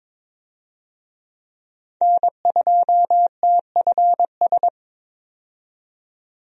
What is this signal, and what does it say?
Morse code, heard as a keyed tone, about 22 wpm, 700 Hz. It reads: N2TFS
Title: USA Call Signs - 5 Characters - 22wpm